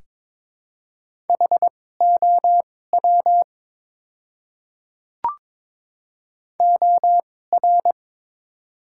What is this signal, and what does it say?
Morse code, heard as a keyed tone, about 22 wpm, spaced as for 16 wpm, 700 Hz. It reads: HOW OR
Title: 2x Character Spacing - Top 100 Words - 22wpm